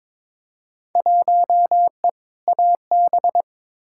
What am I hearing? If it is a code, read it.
Morse code, heard as a keyed tone, about 22 wpm, 700 Hz. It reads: 1E AB
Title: Field Day Exchanges - 22wpm